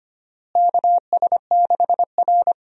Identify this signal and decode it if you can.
Morse code, heard as a keyed tone, about 25 wpm, 700 Hz. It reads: KS6R
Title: USA Call Signs - 4 Characters - Repeated 2x - 25wpm